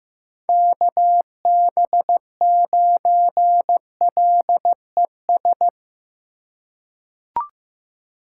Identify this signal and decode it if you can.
Morse code, heard as a keyed tone, about 16 wpm, 700 Hz. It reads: KB9LES
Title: USA Call Signs - 3x Character Spacing - 15wpm